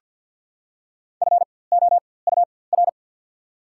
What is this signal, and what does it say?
Morse code, heard as a keyed tone, about 48 wpm, spaced as for 23 wpm, 690 Hz. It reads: FOUR
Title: 4x Character Spacing - Top 500 Words - 50wpm